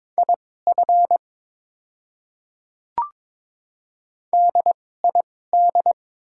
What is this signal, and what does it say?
Morse code, heard as a keyed tone, about 21 wpm, spaced as for 16 wpm, 690 Hz. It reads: IF DID
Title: 2x Character Spacing - Top 100 Words - 22wpm